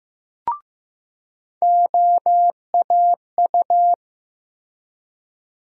Morse code, heard as a keyed tone, about 15 wpm, 700 Hz. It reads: OAU
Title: USA Call Sign Postfixes - 3 Letters 15wpm